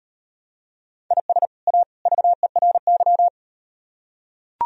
Morse code, heard as a keyed tone, about 36 wpm, 700 Hz. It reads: IS A VERY